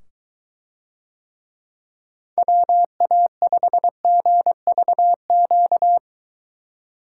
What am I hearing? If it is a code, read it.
Morse code, heard as a keyed tone, about 23 wpm, 700 Hz. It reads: WA5GVQ